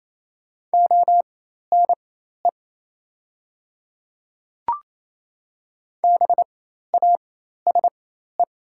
Morse code, heard as a keyed tone, about 28 wpm, spaced as for 13 wpm, 700 Hz. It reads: ONE BASE
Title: 4x Character Spacing - Top 500 Words - 28wpm